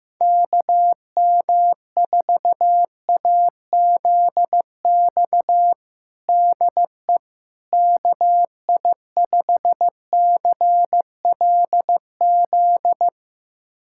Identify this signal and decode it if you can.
Morse code, heard as a keyed tone, about 15 wpm, 700 Hz. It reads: KM4AZX DE KI5CLZ
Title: USA Callsign DE Callsign - 6 Characters - 15wpm